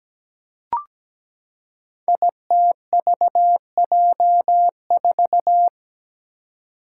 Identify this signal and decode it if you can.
Morse code, heard as a keyed tone, about 17 wpm, 700 Hz. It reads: ITVJ4
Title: Five Challenging Letter-Number Combinations 17wpm